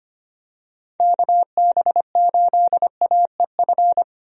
Morse code, heard as a keyed tone, about 25 wpm, 700 Hz. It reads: KB8AEF
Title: USA Call Signs - 6 Characters - 25wpm